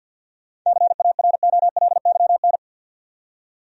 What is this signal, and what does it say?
Morse code, heard as a keyed tone, about 53 wpm, 700 Hz. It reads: CAROLYN
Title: USA Common Names - 50wpm